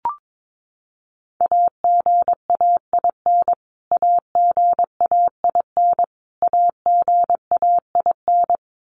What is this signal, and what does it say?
Morse code, heard as a keyed tone, about 22 wpm, 700 Hz. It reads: AGAIN AGAIN AGAIN
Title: Top 500 Words Repeated 22wpm